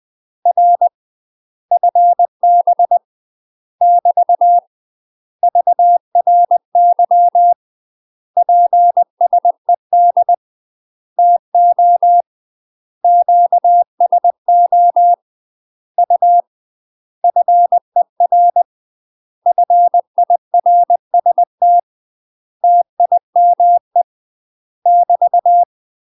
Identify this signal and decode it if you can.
Morse code, heard as a keyed tone, about 20 wpm, 700 Hz. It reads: R FB <BT> VRY PSED TO QSO U FER FIRST TIME <BT>
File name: UK MASTER QSOs TRAINING / 20WPM / UK MASTER QSOs 0018